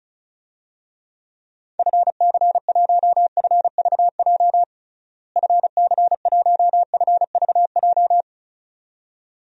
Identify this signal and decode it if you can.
Morse code, heard as a keyed tone, about 34 wpm, 700 Hz. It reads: FC1FVJ FC1FVJ
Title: DXCC-100 Call Signs - 6 Characters - Repeated 2x - 35wpm